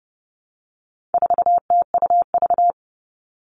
Morse code, heard as a keyed tone, about 30 wpm, 700 Hz. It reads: 4TUV
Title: All Cut Numbers - 4-Digit 30wpm